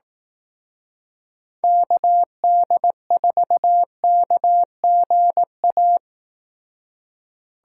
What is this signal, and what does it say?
Morse code, heard as a keyed tone, about 18 wpm, 700 Hz. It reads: KD4KGA
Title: Speed Racing - USA 2x3 Call Signs 15wpm